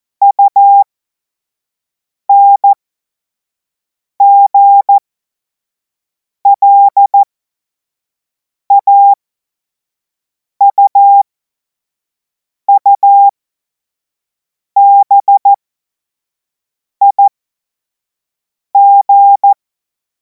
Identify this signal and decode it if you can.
Morse code, heard as a keyed tone, about 14 wpm, spaced as for 5 wpm, 800 Hz. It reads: UNGLAUUBIG
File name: Morsekurs / Witze / Witz 17. Februar 2026 25 F70